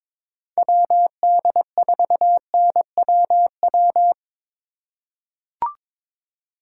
Morse code, heard as a keyed tone, about 22 wpm, 700 Hz. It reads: WD4NWW
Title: USA Call Signs - 6 Characters - 2x Character Spacing - 22wpm